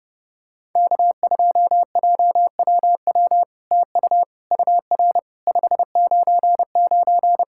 Morse code, heard as a keyed tone, about 32 wpm, 700 Hz. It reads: K2JWW TU UR 599
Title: SOTA QSO Practice 30wpm